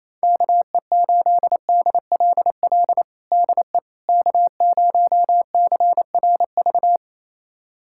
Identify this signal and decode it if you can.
Morse code, heard as a keyed tone, about 28 wpm, 700 Hz. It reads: KE8DLL DE K0CRV
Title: USA Callsign DE Callsign 28wpm